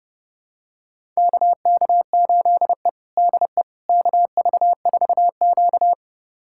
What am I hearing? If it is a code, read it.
Morse code, heard as a keyed tone, about 30 wpm, 700 Hz. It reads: KK8E DE KV4Q
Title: USA Callsign DE Callsign - 4 Characters - 30wpm